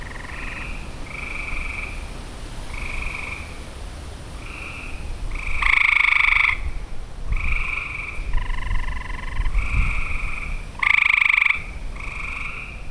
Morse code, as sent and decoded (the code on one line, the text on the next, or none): none